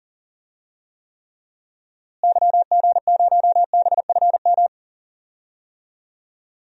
Morse code, heard as a keyed tone, about 40 wpm, 700 Hz. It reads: YG0BFM
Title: DXCC-100 Call Signs - 6 Characters - Repeated 2x - 40wpm